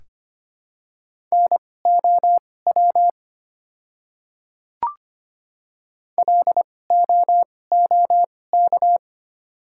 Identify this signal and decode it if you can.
Morse code, heard as a keyed tone, about 25 wpm, spaced as for 18 wpm, 700 Hz. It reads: NOW LOOK
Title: 2x Character Spacing - Top 100 Words - 25wpm